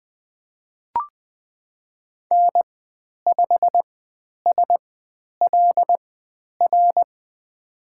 Morse code, heard as a keyed tone, about 20 wpm, spaced as for 10 wpm, 700 Hz. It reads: N5SLR
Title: 095 Lesson 16g - Letter c - Call Signs 20wpm